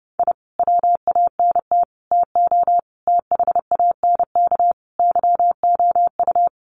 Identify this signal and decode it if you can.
Morse code, heard as a keyed tone, about 30 wpm, 700 Hz. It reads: I WANT TO THANK YOU
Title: Sets of 5 Words Encore 30wpm